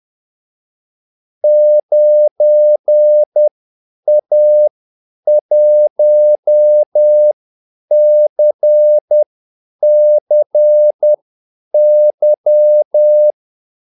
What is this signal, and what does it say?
Morse code, heard as a keyed tone, about 10 wpm, spaced as for 8 wpm, 600 Hz. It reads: 9A1CCY
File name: Callsigns-10-8wpm